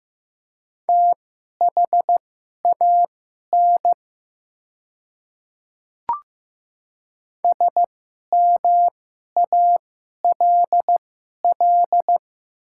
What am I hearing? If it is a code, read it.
Morse code, heard as a keyed tone, about 15 wpm, spaced as for 11 wpm, 700 Hz. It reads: THAN SMALL
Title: 2x Character Spacing - Top 500 Words - 15wpm